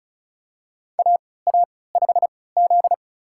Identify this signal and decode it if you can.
Morse code, heard as a keyed tone, about 36 wpm, spaced as for 20 wpm, 700 Hz. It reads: AA5Z
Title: USA Call Signs - 3x Character Spacing - 35wpm